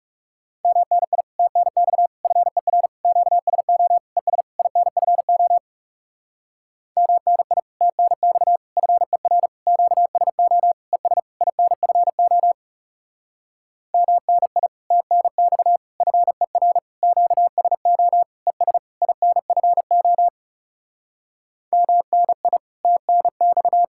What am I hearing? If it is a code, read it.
Morse code, heard as a keyed tone, about 37 wpm, 700 Hz. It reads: MNI TNX FER QSO ES INFO MNI TNX FER QSO ES INFO MNI TNX FER QSO ES INFO MNI TNX
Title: Speed-Racing Common QSO Phrases - 30wpm